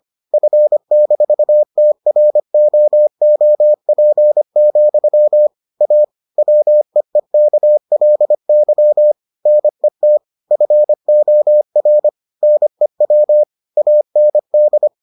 Morse code, heard as a keyed tone, about 25 wpm, 600 Hz. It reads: F-TROOP, A WEEKLY NET FOR NEW AND